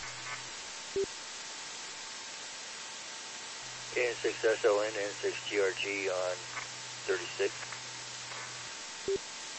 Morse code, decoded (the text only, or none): none